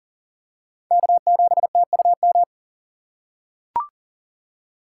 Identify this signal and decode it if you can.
Morse code, heard as a keyed tone, about 40 wpm, 700 Hz. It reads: K7TUM
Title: USA Call Signs - 5 Characters - 2x Character Spacing - 40wpm